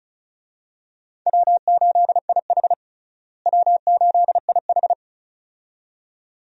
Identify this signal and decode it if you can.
Morse code, heard as a keyed tone, about 34 wpm, 700 Hz. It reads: W8IH W8IH
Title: USA Call Signs - 4 Characters - Repeated 2x - 35wpm